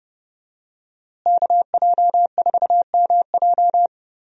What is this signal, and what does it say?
Morse code, heard as a keyed tone, about 30 wpm, 700 Hz. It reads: KJ4MJ
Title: USA Call Signs - 5 Characters - Repeated 2x - 30wpm